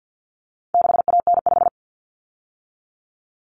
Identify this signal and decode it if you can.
Morse code, heard as a keyed tone, about 50 wpm, 700 Hz. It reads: 6AN5
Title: Common Cut Numbers - 4-Digit 50wpm